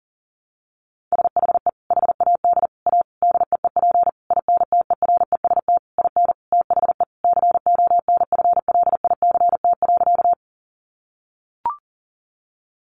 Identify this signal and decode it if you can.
Morse code, heard as a keyed tone, about 40 wpm, 700 Hz. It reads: SHE HAD A DEEP INTEREST IN THE CONFLICT.
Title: Sentences from Top 1300 Words 40wpm